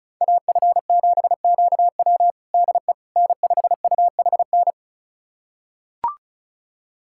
Morse code, heard as a keyed tone, about 34 wpm, 700 Hz. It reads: AF7QW DE N5UHN
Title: USA Callsign DE Callsign - 5 Characters - 35wpm